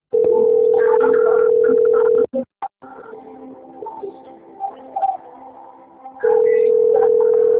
Ringback tone (North America), 440 + 480 Hz.